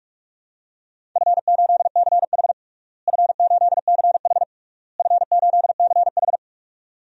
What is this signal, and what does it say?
Morse code, heard as a keyed tone, about 46 wpm, 700 Hz. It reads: F8CH F8CH F8CH
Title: DXCC-100 Call Signs - 4 Characters - Repeated 3x - 45wpm